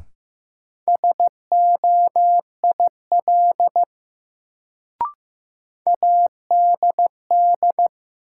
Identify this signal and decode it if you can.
Morse code, heard as a keyed tone, about 15 wpm, 700 Hz. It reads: SOIL ADD
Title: Sentences from Top 1500 Words - New Word Drill Alt - 15wpm